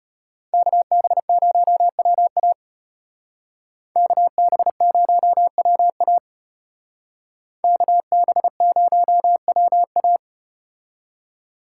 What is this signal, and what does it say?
Morse code, heard as a keyed tone, about 34 wpm, 700 Hz. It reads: KB0WA KB0WA KB0WA